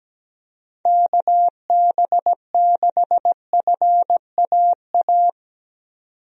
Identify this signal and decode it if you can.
Morse code, heard as a keyed tone, about 17 wpm, 700 Hz. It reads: KB6FAA
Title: USA Call Signs - 6 Characters - 17wpm